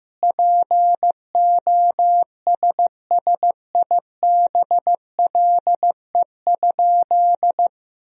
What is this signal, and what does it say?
Morse code, heard as a keyed tone, about 16 wpm, 700 Hz. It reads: POSSIBLE?
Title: Sentences from Top 500 Words and Abbreviations 15wpm